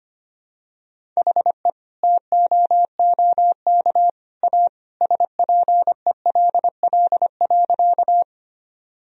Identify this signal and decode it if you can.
Morse code, heard as a keyed tone, about 25 wpm, 700 Hz. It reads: HE TOOK A SPELL.
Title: Sentences from Top 1000 Words - Review Part 1 - 25wpm